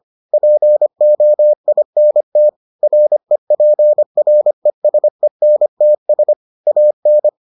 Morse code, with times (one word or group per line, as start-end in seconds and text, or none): 0.3-2.5 POINT
2.8-6.3 REPRESENTS
6.7-7.3 AN